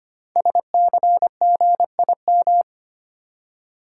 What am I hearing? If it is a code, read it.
Morse code, heard as a keyed tone, about 25 wpm, 700 Hz. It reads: SCGIM
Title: Code Groups - 5 Letter:Numbers - 25wpm